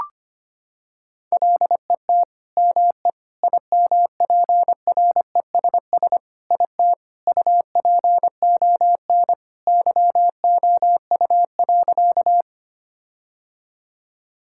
Morse code, heard as a keyed tone, about 25 wpm, 700 Hz. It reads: LET ME IMPRESS IT UPON YOU.
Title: Sentences from Top 1600 Words 25wpm